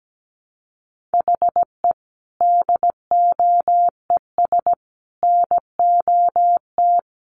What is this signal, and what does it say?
Morse code, heard as a keyed tone, about 17 wpm, 700 Hz. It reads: HE DOES NOT
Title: Sets of 3 Words 17wpm